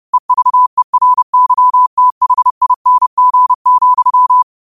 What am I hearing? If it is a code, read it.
Morse code, heard as a keyed tone, about 30 wpm, 1000 Hz. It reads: EVERYTHING,
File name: qotdfiles / 30wpm